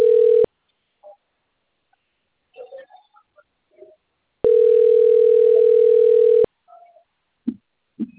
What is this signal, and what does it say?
Ringback tone (North America), 440 + 480 Hz.